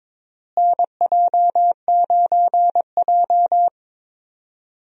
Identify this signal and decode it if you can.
Morse code, heard as a keyed tone, about 22 wpm, 700 Hz. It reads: NJ9J
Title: USA Call Signs - 4 Characters - Repeated 2x - 22wpm